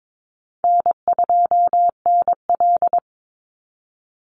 Morse code, heard as a keyed tone, about 22 wpm, 700 Hz. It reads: N2NL
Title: Easy Call Signs Repeated 3x - 22wpm